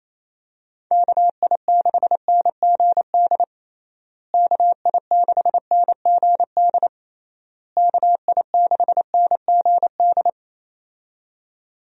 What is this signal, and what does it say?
Morse code, heard as a keyed tone, about 28 wpm, 700 Hz. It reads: KI6NGD KI6NGD KI6NGD
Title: USA Call Signs - 6 Characters - Repeated 3x - 28wpm